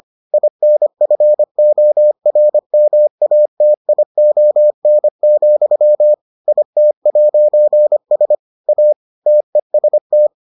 Morse code, heard as a keyed tone, about 25 wpm, 600 Hz. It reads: INFORMATION, IT'S A TEST